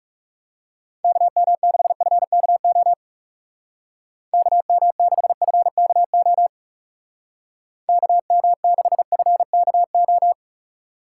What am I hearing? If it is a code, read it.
Morse code, heard as a keyed tone, about 35 wpm, 700 Hz. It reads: KM6FKO KM6FKO KM6FKO
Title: Speed Racing - USA Call Signs 30wpm